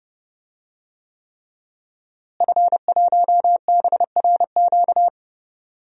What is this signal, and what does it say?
Morse code, heard as a keyed tone, about 30 wpm, 700 Hz. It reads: F1BRQ
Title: DXCC-100 Call Signs - 5 Characters - Repeated 3x - 30wpm